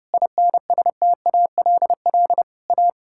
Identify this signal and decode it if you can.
Morse code, heard as a keyed tone, about 32 wpm, 700 Hz. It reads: INSTALL A
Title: Sentences from Top 1600 Words 30wpm